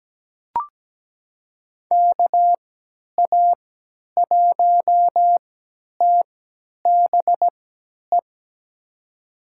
Morse code, heard as a keyed tone, about 17 wpm, spaced as for 10 wpm, 700 Hz. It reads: KA1TBE
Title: USA Call Signs - 6 Characters - 3x Character Spacing - 17wpm